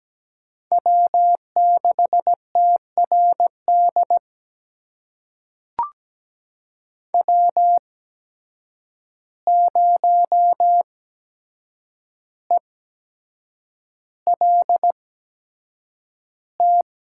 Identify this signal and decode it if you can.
Morse code, heard as a keyed tone, about 17 wpm, spaced as for 7 wpm, 700 Hz. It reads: W6TRD W0ELT
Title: USA Call Signs - 8x Character Spacing - 17wpm